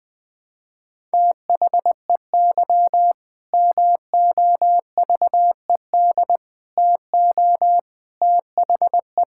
Morse code, heard as a keyed tone, about 20 wpm, 700 Hz. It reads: THEY MOVED TO THE
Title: Sentences with 4-Word Phrases - 20wpm